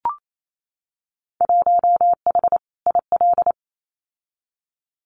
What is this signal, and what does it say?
Morse code, heard as a keyed tone, about 28 wpm, 700 Hz. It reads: 1H IL
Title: Winter Field Day Exchanges 28wpm